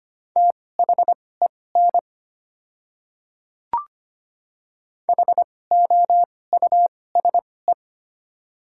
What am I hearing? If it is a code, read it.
Morse code, heard as a keyed tone, about 25 wpm, spaced as for 18 wpm, 690 Hz. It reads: THEN HOUSE
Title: 2x Character Spacing - Top 200 Words - 25wpm